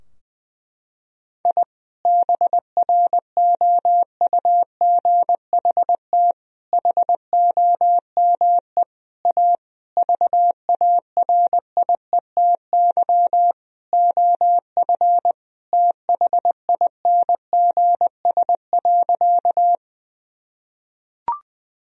Morse code, heard as a keyed tone, about 20 wpm, 700 Hz. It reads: I BROUGHT HOME A VARIETY OF THINGS.
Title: Sentences from Top 1000 Words - Review Part 4 - 20wpm